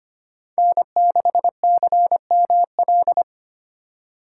Morse code, heard as a keyed tone, about 25 wpm, 700 Hz. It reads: N6CML
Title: USA Call Signs - 5 Characters - 4x Character Spacing - 25wpm